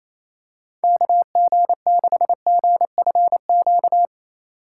Morse code, heard as a keyed tone, about 27 wpm, 700 Hz. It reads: KG6GFQ
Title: USA Call Signs - 6 Characters - 3x Character Spacing - 28wpm